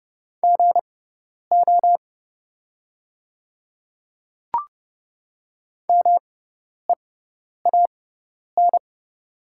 Morse code, heard as a keyed tone, about 30 wpm, spaced as for 10 wpm, 700 Hz. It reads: GO MEAN
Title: 6x Character Spacing - Top 500 Words - 30wpm